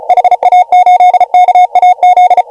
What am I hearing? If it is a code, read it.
Morse code, heard as a keyed tone, about 36 wpm, 690 Hz. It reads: HA8KAZ